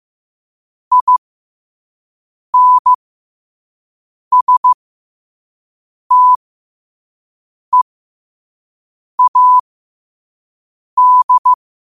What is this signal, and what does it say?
Morse code, heard as a keyed tone, about 15 wpm, spaced as for 5 wpm, 1000 Hz. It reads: INSTEAD
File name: qotdfiles / 5wpm